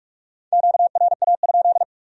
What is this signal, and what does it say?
Morse code, heard as a keyed tone, about 58 wpm, 700 Hz. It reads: QRA?